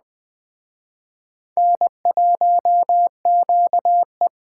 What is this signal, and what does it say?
Morse code, heard as a keyed tone, about 19 wpm, 700 Hz. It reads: N1QE